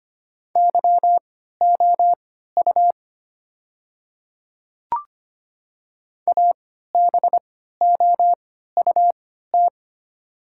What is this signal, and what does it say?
Morse code, heard as a keyed tone, about 25 wpm, spaced as for 14 wpm, 700 Hz. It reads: YOU ABOUT
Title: 3x Character Spacing - Top 200 Words - 25wpm